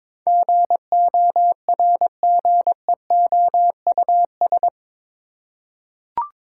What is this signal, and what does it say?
Morse code, heard as a keyed tone, about 22 wpm, 700 Hz. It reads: GORGEOUS
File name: practice / 22wpm / Eight Letter Words - 22wpm